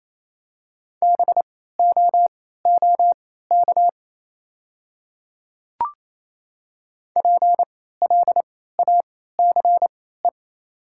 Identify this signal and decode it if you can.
Morse code, heard as a keyed tone, about 27 wpm, spaced as for 16 wpm, 690 Hz. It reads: BOOK PLACE
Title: 3x Character Spacing - Top 500 Words - 28wpm